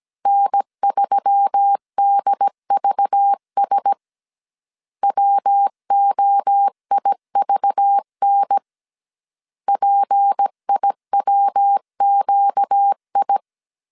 Morse code, heard as a keyed tone, about 17 wpm, 800 Hz. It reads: N3DVS WOIVN PIWQI
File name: CW / DET / 1 6 1500C S B 16wpm 2619